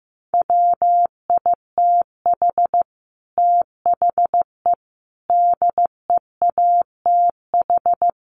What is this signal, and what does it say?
Morse code, heard as a keyed tone, about 15 wpm, 700 Hz. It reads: WITH THE DEATH
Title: A Sound Of Thunder 15wpm